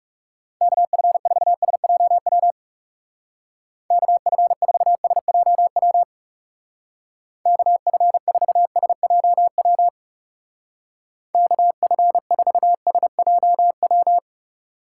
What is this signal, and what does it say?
Morse code, heard as a keyed tone, about 36 wpm, 700 Hz. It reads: KF4SJW KF4SJW KF4SJW KF4SJW
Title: Speed Racing - USA Call Signs 30wpm